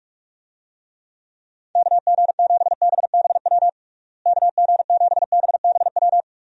Morse code, heard as a keyed tone, about 41 wpm, 700 Hz. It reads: KG7BBW KG7BBW
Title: USA Call Signs - 6 Characters - Repeated 3x - 45wpm